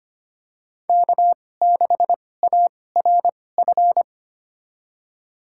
Morse code, heard as a keyed tone, about 25 wpm, spaced as for 18 wpm, 700 Hz. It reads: K6ARF
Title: USA Call Signs - 5 Characters - 2x Character Spacing - 25wpm